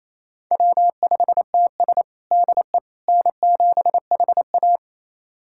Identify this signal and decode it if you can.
Morse code, heard as a keyed tone, about 27 wpm, 700 Hz. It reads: W5TS DE N7HA
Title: USA Callsign DE Callsign - 4 Characters - 28wpm